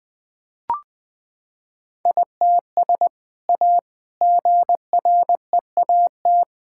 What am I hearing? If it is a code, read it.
Morse code, heard as a keyed tone, about 21 wpm, 700 Hz. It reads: ITS A GREAT
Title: Sentences from Top 1000 Words - Review Part 6 - 20wpm